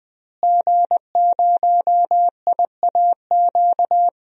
Morse code, heard as a keyed tone, about 20 wpm, 700 Hz. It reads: G0IAQ
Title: DXCC-100 Call Signs - 5 Characters - Repeated 3x - 20wpm